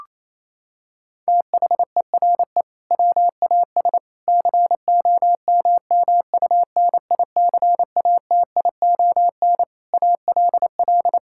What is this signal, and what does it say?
Morse code, heard as a keyed tone, about 28 wpm, 700 Hz. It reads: THERE WAS COMMUNICATION ALL
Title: Sentences from Top 1000 Words - Review Part 6 - 28wpm